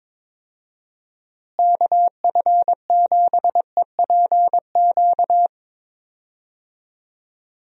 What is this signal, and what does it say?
Morse code, heard as a keyed tone, about 22 wpm, 700 Hz. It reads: KF7EPQ
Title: USA Call Signs - 6 Characters - Repeated 2x - 22wpm